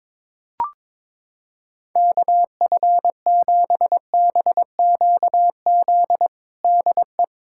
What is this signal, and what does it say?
Morse code, heard as a keyed tone, about 22 wpm, 700 Hz. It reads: KF7BQZ DE
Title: USA Callsign DE Callsign - 6 Characters - 22wpm